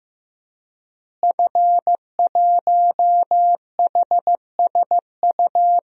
Morse code, heard as a keyed tone, about 15 wpm, 700 Hz. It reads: F1HSU
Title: DXCC-100 Call Signs - 5 Characters - Repeated 2x - 15wpm